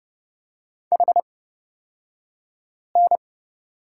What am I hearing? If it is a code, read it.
Morse code, heard as a keyed tone, about 30 wpm, spaced as for 5 wpm, 700 Hz. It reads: HN
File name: Lesson 38d - Letter q - Two Characters 30wpm